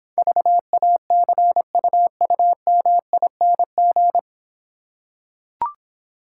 Morse code, heard as a keyed tone, about 26 wpm, 700 Hz. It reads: VACUUMING